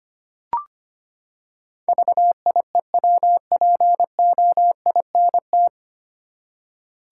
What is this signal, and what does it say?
Morse code, heard as a keyed tone, about 25 wpm, 700 Hz. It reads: VIEWPOINT
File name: Nine Letter Words - 25wpm